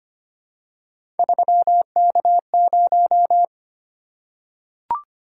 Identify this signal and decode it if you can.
Morse code, heard as a keyed tone, about 25 wpm, 700 Hz. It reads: 3K0
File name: Three Letter-Number Combinations - 25wpm